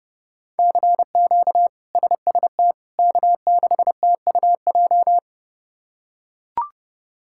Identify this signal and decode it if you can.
Morse code, heard as a keyed tone, about 30 wpm, 700 Hz. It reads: CQ SST K6TUJ
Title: K1USN SST QSO Practice - 30wpm